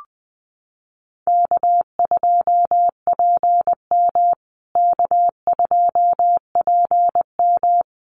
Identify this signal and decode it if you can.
Morse code, heard as a keyed tone, about 20 wpm, 700 Hz. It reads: K2PM K2PM
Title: USA Call Signs Repeated 3x 20wpm